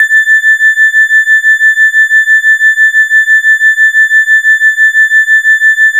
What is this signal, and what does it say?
There is no code.